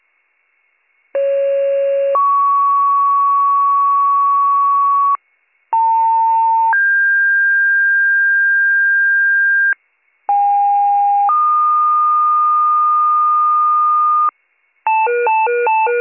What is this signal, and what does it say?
Two-tone page, heard as a one-second tone, then a three-second tone, about 550 then 1120 Hz, about 900 then 1600 Hz, about 810 then 1220 Hz. It is heard from 1.1 to 14.3 s.